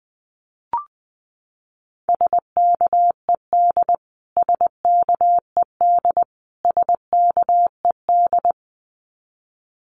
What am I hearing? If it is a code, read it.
Morse code, heard as a keyed tone, about 20 wpm, 700 Hz. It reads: SKED SKED SKED
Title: Abbreviations Repeated 3x - 20wpm